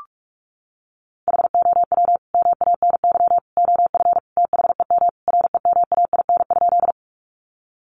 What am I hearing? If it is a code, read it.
Morse code, heard as a keyed tone, about 44 wpm, 700 Hz. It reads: HOW MANY OF THEM REMAIN?
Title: Sentences from Top 1200 Words 45wpm